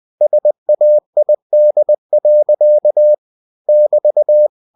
Morse code, heard as a keyed tone, about 20 wpm, 600 Hz. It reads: SAID. <BT>
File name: cw-books / winnie-the-pooh / winnie-20wpm-page-0106